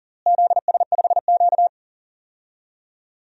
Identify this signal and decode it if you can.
Morse code, heard as a keyed tone, about 40 wpm, 700 Hz. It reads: ZS5Q
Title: DXCC-100 Call Signs - 4 Characters - Repeated 2x - 40wpm